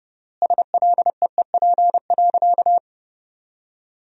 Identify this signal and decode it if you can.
Morse code, heard as a keyed tone, about 32 wpm, 700 Hz. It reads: SLEEP.
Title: 3x Word-Spacing - Sentences from Top 500 Words - 30wpm